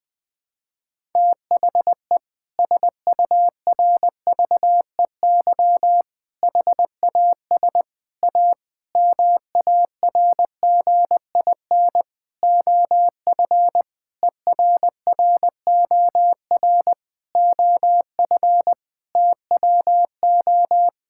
Morse code, heard as a keyed tone, about 21 wpm, 700 Hz. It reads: THE SURVEY HAS A MARGIN OF ERROR OF TWO
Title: Sentences with 6-Word Phrases - 20wpm